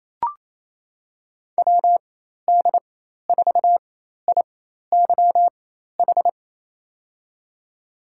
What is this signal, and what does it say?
Morse code, heard as a keyed tone, about 28 wpm, spaced as for 13 wpm, 700 Hz. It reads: WD4IYH
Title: USA Call Signs - 6 Characters - 4x Character Spacing - 28wpm